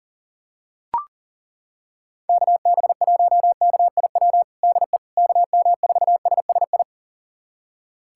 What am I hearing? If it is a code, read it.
Morse code, heard as a keyed tone, about 40 wpm, 700 Hz. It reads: KB1KIW DE KM4SSI
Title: USA Callsign DE Callsign 40wpm